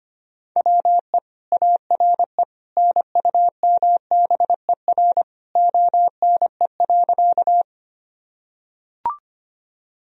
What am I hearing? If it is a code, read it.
Morse code, heard as a keyed tone, about 25 wpm, 700 Hz. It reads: WE ARE NUMBER ONE.
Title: Sentences from Top 1000 Words - Review Part 1 - 25wpm